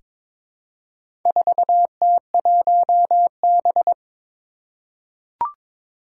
Morse code, heard as a keyed tone, about 21 wpm, 700 Hz. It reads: 4T1B
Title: Four Challenging Letter-Number Combinations 22wpm